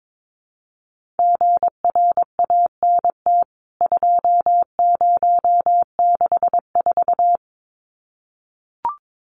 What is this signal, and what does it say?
Morse code, heard as a keyed tone, about 22 wpm, 700 Hz. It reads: GRANT 2064
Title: CWT Exchange 22wpm